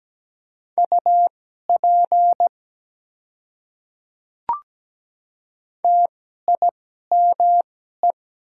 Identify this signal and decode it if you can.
Morse code, heard as a keyed tone, about 17 wpm, spaced as for 12 wpm, 700 Hz. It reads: UP TIME
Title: 2x Character Spacing - Top 200 Words - 17wpm